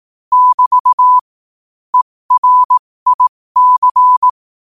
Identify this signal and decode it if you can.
Morse code, heard as a keyed tone, about 18 wpm, spaced as for 15 wpm, 1000 Hz. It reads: <BT> ERIC
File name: qotdfiles / 15wpm